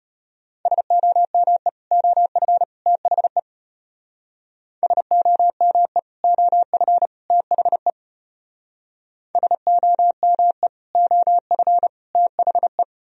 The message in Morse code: SOME OF THE SOME OF THE SOME OF THE

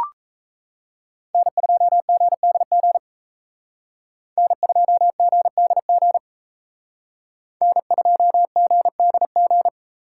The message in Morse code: N2GDG N2GDG N2GDG

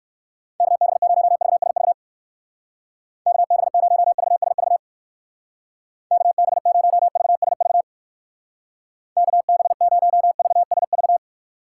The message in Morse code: KB0VSV KB0VSV KB0VSV KB0VSV

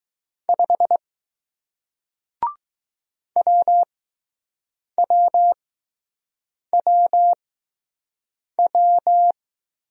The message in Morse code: 5 W W W W

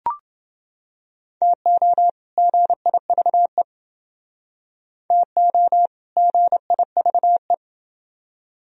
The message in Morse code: TO GIVE TO GIVE

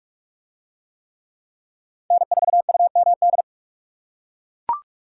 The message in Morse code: N4UMD

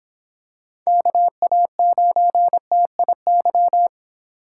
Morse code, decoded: KA9TIY

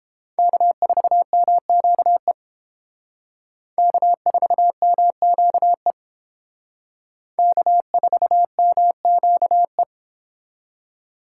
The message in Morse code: K4MQE K4MQE K4MQE

